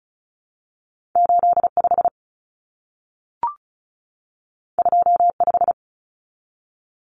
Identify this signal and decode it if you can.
Morse code, heard as a keyed tone, about 36 wpm, 700 Hz. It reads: 85 25
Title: Common Cut Numbers - 2-Digit 35wpm